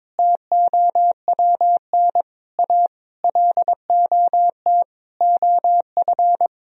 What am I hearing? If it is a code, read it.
Morse code, heard as a keyed tone, about 21 wpm, 700 Hz. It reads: TOWN A LOT OF
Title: Sentences from Top 1000 Words - Review Part 3 - 22wpm